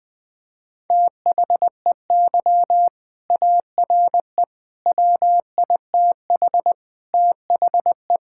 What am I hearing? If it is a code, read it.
Morse code, heard as a keyed tone, about 21 wpm, 700 Hz. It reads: THEY ARE WITH THE
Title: Sentences from Top 1000 Words - Review Part 5 - 20wpm